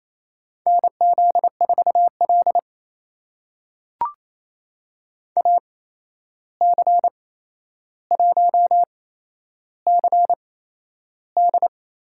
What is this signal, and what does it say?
Morse code, heard as a keyed tone, about 28 wpm, spaced as for 11 wpm, 700 Hz. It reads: NZ4L AC1CD
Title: USA Call Signs - 8x Character Spacing - 28wpm